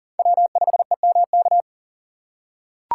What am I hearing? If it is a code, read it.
Morse code, heard as a keyed tone, about 40 wpm, 700 Hz. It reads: W5EMK
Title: USA Call Signs - 5 Characters - 3x Character Spacing - 40wpm